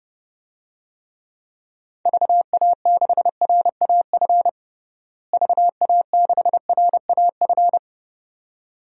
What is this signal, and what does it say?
Morse code, heard as a keyed tone, about 30 wpm, 700 Hz. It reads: VA6RAF VA6RAF
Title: DXCC-100 Call Signs - 6 Characters - Repeated 2x - 30wpm